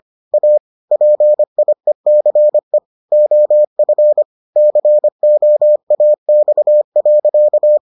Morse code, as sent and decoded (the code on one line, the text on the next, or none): .- / .--. .. -.-. / --- ..-. / -.-. --- .- -..- .-.-.-
A PIECE OF COAX.